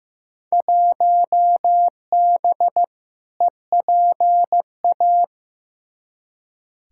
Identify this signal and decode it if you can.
Morse code, heard as a keyed tone, about 15 wpm, 700 Hz. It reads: 1B EPA
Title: Field Day Exchanges - 15wpm